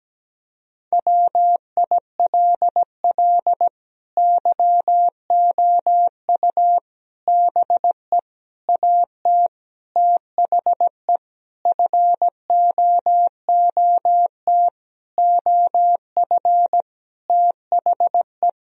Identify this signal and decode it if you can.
Morse code, heard as a keyed tone, about 17 wpm, 700 Hz. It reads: WILL YOU BE AT THE FOOT OF THE
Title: Sentences from Top 1000 Words - Review Part 3 - 17wpm